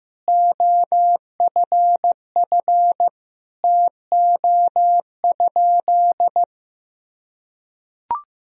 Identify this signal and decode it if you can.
Morse code, heard as a keyed tone, about 15 wpm, 700 Hz. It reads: OFF TO?
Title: Sentences from Top 200 Words and Abbreviations 15wpm